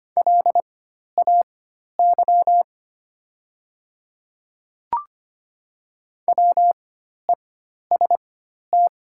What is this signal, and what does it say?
Morse code, heard as a keyed tone, about 25 wpm, spaced as for 12 wpm, 690 Hz. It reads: LAY WEST
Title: 4x Character Spacing - Top 500 Words - 25wpm